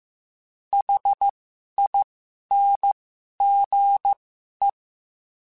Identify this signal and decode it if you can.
Morse code, heard as a keyed tone, about 15 wpm, spaced as for 11 wpm, 800 Hz. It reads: HINGE